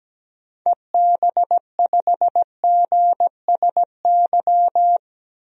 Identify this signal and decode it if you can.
Morse code, heard as a keyed tone, about 17 wpm, 700 Hz. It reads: EB5GSY
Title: DXCC-100 Call Signs - 6 Characters - 17wpm